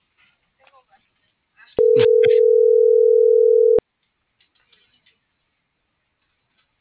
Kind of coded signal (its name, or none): telephone call progress tone